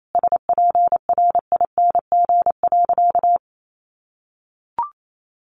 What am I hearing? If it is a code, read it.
Morse code, heard as a keyed tone, about 28 wpm, 700 Hz. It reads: SPRING.
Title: Sentences from Top 1200 Words 28wpm